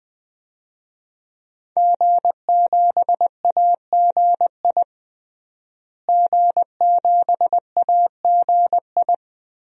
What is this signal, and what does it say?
Morse code, heard as a keyed tone, about 20 wpm, 700 Hz. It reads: G7AGI G7AGI
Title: DXCC-100 Call Signs - 5 Characters - Repeated 3x - 20wpm